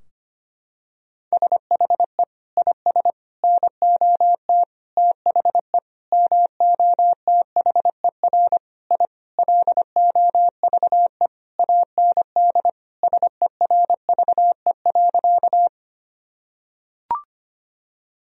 Morse code, heard as a keyed tone, about 25 wpm, 700 Hz. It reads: SHE IS NOT THE MOTHER I LOVE AND SERVE.
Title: Sentences from Top 1000 Words - Review Part 2 - 25wpm